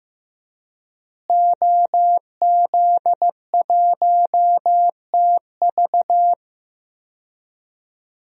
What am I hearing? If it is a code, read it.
Morse code, heard as a keyed tone, about 15 wpm, 700 Hz. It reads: OZ1TV
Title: DXCC-100 Call Signs - 5 Characters - Repeated 3x - 15wpm